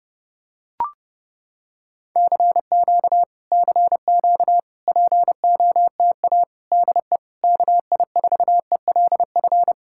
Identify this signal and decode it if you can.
Morse code, heard as a keyed tone, about 30 wpm, 700 Hz. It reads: CQ CQ POTA DE KI4ELF